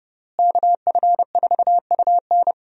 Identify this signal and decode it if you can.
Morse code, heard as a keyed tone, about 30 wpm, 700 Hz. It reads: KF4UN
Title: USA Call Signs - 5 Characters - Repeated 3x - 30wpm